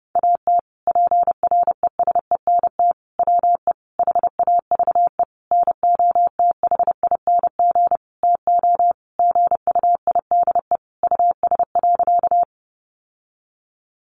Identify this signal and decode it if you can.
Morse code, heard as a keyed tone, about 28 wpm, 700 Hz. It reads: AT PRESENT WE HAVE NOTHING TO GUIDE US.
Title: Sentences from Top 800 Words 30wpm